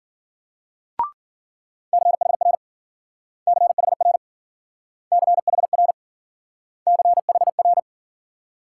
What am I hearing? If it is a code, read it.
Morse code, heard as a keyed tone, about 52 wpm, 700 Hz. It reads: CHR CHR CHR CHR